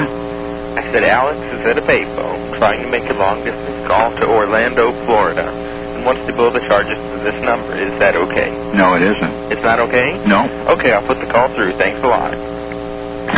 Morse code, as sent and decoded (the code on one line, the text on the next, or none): none